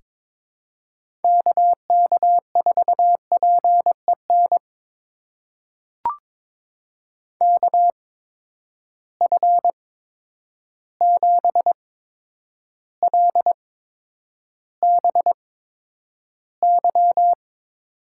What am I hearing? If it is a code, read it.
Morse code, heard as a keyed tone, about 21 wpm, spaced as for 9 wpm, 700 Hz. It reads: KK4PEN KF7LBY
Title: USA Call Signs - 6 Characters - 8x Character Spacing - 22wpm